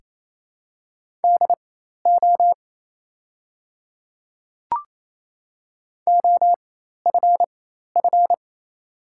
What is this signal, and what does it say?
Morse code, heard as a keyed tone, about 27 wpm, spaced as for 13 wpm, 690 Hz. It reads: DO OFF
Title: 4x Character Spacing - Top 200 Words - 28wpm